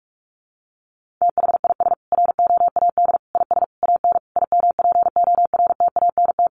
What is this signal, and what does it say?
Morse code, heard as a keyed tone, about 44 wpm, 700 Hz. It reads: THIS ROAD IS AN IMPORTANT
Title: Sentences from Top 1400 Words 45wpm